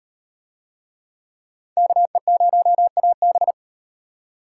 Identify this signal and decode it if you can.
Morse code, heard as a keyed tone, about 37 wpm, 700 Hz. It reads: KE0AB